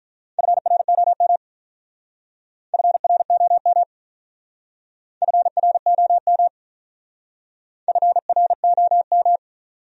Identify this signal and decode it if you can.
Morse code, heard as a keyed tone, about 43 wpm, 700 Hz. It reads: FROM FROM FROM FROM
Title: Speed-Racing Top 200 Words 35wpm